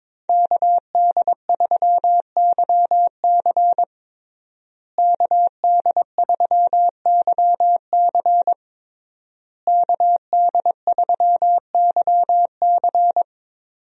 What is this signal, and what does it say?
Morse code, heard as a keyed tone, about 21 wpm, 700 Hz. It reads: KD3YC KD3YC KD3YC
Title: USA Call Signs - 5 Characters - Repeated 3x - 22wpm